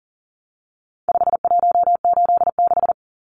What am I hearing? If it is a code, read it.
Morse code, heard as a keyed tone, about 40 wpm, 700 Hz. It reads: 5186
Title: Numbers - 4-Digit 40wpm